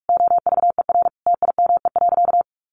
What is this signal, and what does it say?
Morse code, heard as a keyed tone, about 44 wpm, 700 Hz. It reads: OVER TIME.
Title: Sentences from Top 1500 Words 45wpm